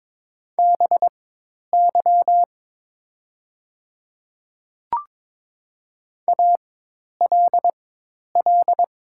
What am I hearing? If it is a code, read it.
Morse code, heard as a keyed tone, about 22 wpm, spaced as for 10 wpm, 700 Hz. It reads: BY ALL
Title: 4x Character Spacing - Top 100 Words - 22wpm